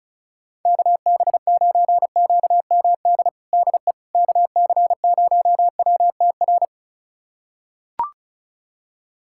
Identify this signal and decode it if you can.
Morse code, heard as a keyed tone, about 36 wpm, 700 Hz. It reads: KB9QMD DE KC0WTR
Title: USA Callsign DE Callsign 35wpm